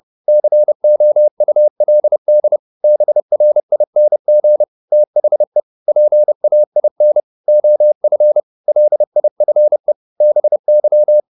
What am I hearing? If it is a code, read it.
Morse code, heard as a keyed tone, about 30 wpm, 600 Hz. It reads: COULD BRING THE PAIN OF LIFE BY